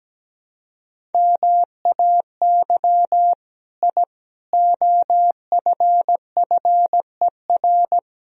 Morse code, heard as a keyed tone, about 17 wpm, 700 Hz. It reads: MAY I OFFER
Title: Sentences from Top 1000 Words - Review Part 4 - 17wpm